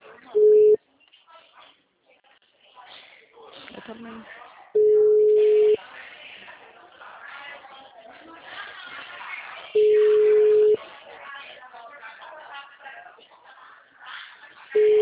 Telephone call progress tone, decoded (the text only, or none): ringback tone